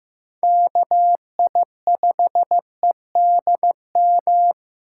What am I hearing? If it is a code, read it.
Morse code, heard as a keyed tone, about 15 wpm, 700 Hz. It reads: KI5EDM
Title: USA Call Signs - 4x Character Spacing - 15wpm